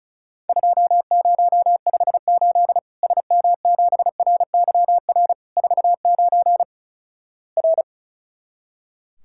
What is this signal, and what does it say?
Morse code, heard as a keyed tone, about 36 wpm, 700 Hz. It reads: 2058 SM7RYR 49 R
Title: Worked All Europe DX Contest - QTC Exchange - 35wpm